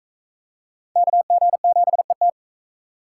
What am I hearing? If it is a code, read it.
Morse code, heard as a keyed tone, about 39 wpm, 700 Hz. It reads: KG7ET